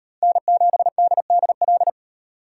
Morse code, heard as a keyed tone, about 38 wpm, 700 Hz. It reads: N7DDL